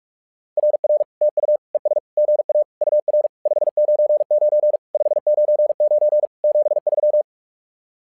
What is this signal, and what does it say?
Morse code, heard as a keyed tone, about 44 wpm, 600 Hz. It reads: RR TU ES GA UR 599 599 73
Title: SOTA QSO Practice 45wpm